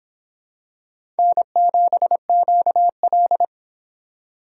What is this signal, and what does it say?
Morse code, heard as a keyed tone, about 25 wpm, 700 Hz. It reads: N7QL